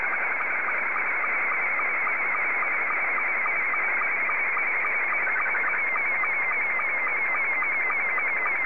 SSTV (slow-scan television), Scottie 2.
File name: sstv / S2-ZL2TJM